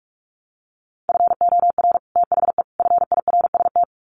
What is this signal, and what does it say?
Morse code, heard as a keyed tone, about 44 wpm, 700 Hz. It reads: FOR THE FIRST
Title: Sets of 3 Words 45wpm